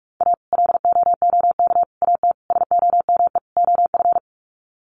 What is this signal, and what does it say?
Morse code, heard as a keyed tone, about 44 wpm, 700 Hz. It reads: A LOOK AT SOME OF
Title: Sets of 5 Words Encore2 45wpm